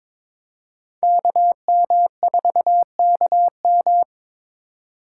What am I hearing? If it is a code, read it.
Morse code, heard as a keyed tone, about 26 wpm, 700 Hz. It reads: KM4KM
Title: Speed Racing - USA Call Signs 22wpm